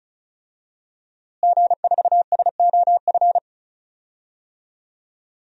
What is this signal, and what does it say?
Morse code, heard as a keyed tone, about 34 wpm, 700 Hz. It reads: G4SOF
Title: DXCC-100 Call Signs - 5 Characters - Repeated 2x - 35wpm